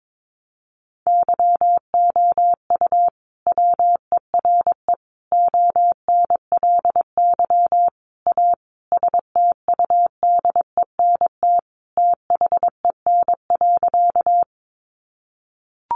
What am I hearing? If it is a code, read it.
Morse code, heard as a keyed tone, about 22 wpm, 700 Hz. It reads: YOU WERE ONLY A STUDENT THEN.
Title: Sentences from Top 1000 Words 22wpm